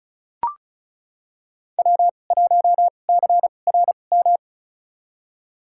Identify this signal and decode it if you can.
Morse code, heard as a keyed tone, about 34 wpm, spaced as for 25 wpm, 700 Hz. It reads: W1CRM
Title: USA Call Signs - 2x Character Spacing - 35wpm